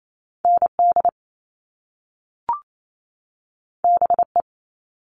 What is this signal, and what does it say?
Morse code, heard as a keyed tone, about 28 wpm, 700 Hz. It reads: ND BE
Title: All Cut Numbers - 2-Digit 28wpm